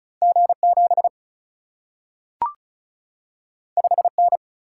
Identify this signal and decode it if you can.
Morse code, heard as a keyed tone, about 36 wpm, 700 Hz. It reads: G 7 5 N
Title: Single-Letter Single-Number 35wpm